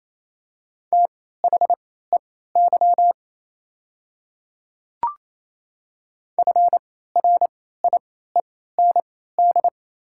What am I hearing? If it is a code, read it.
Morse code, heard as a keyed tone, about 28 wpm, spaced as for 16 wpm, 700 Hz. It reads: THEY FRIEND
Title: 3x Character Spacing - Top 300 Words - 28wpm